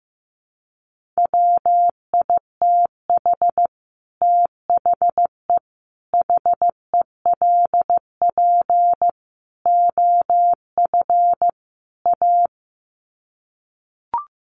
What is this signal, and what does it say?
Morse code, heard as a keyed tone, about 15 wpm, 700 Hz. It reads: WITH THE HELP OF A
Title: Sets of 5 Words 15wpm